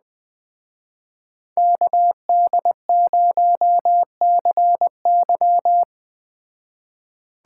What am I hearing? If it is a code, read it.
Morse code, heard as a keyed tone, about 21 wpm, 700 Hz. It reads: KD0CY
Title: Speed Racing - USA 2x2 Call Signs 17wpm